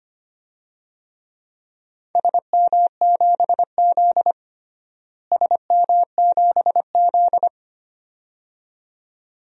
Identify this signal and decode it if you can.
Morse code, heard as a keyed tone, about 25 wpm, 700 Hz. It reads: SM7Z SM7Z
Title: DXCC-100 Call Signs - 4 Characters - Repeated 2x - 25wpm